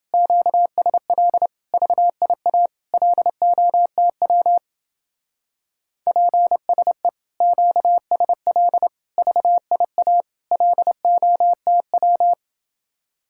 Morse code, heard as a keyed tone, about 29 wpm, 700 Hz. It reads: QSL VIA LOTW PSE QSL VIA LOTW